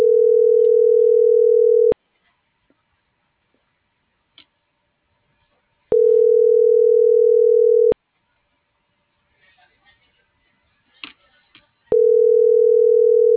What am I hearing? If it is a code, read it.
Ringback tone (North America), 440 + 480 Hz.